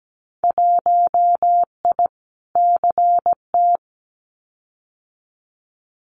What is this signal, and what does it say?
Morse code, heard as a keyed tone, about 17 wpm, 700 Hz. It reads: 1I CT
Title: Winter Field Day Exchanges 17wpm